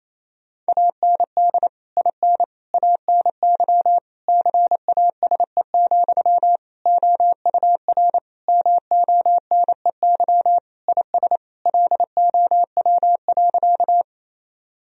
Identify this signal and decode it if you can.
Morse code, heard as a keyed tone, about 28 wpm, 700 Hz. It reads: AND IN ANY CASE, OUR MONEY IS LOW.
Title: Sentences from Top 1000 Words - Review Part 1 - 28wpm